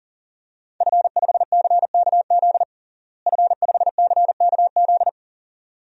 Morse code, heard as a keyed tone, about 40 wpm, 700 Hz. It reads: F5CKZ F5CKZ
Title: DXCC-100 Call Signs - 5 Characters - Repeated 2x - 40wpm